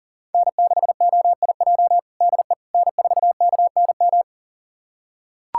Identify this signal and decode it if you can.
Morse code, heard as a keyed tone, about 40 wpm, 700 Hz. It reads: N6OIJ DE N4KNM
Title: USA Callsign DE Callsign - 5 Characters - 40wpm